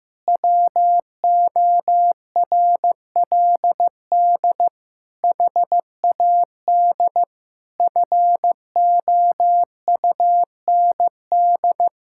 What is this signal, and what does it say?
Morse code, heard as a keyed tone, about 15 wpm, 700 Hz. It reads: WORLD HAD FOUND
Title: Sentences from Top 1600 Words 15wpm